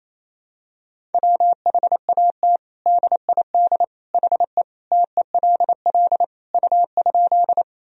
Morse code, heard as a keyed tone, about 28 wpm, 700 Hz. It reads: WHAT DID HE TELL U?
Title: Sentences from Top 200 Words and Abbreviations 28wpm